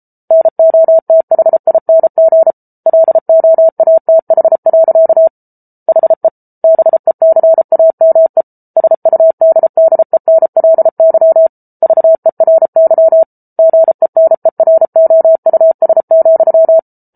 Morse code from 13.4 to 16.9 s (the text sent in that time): GENEROUS,